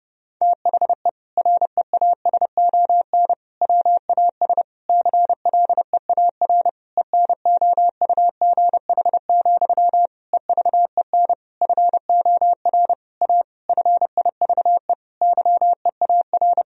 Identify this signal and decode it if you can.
Morse code, heard as a keyed tone, about 32 wpm, 700 Hz. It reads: THE REASON WAS CLEAR ENOUGH, EVEN FOR A FIVE YEAR
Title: Sentences from Top 1000 Words - Review Part 2 - 30wpm